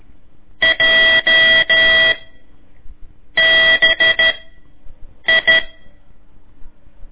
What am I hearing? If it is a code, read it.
Morse code, heard as a keyed tone, about 13 wpm, 1950 Hz. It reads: J B I
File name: ik1weg2 / CORSI CW / lez4-1